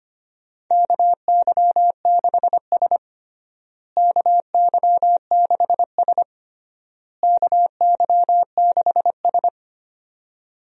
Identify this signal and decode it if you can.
Morse code, heard as a keyed tone, about 25 wpm, 700 Hz. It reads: KY6S KY6S KY6S
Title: USA Call Signs - 4 Characters - Repeated 3x - 25wpm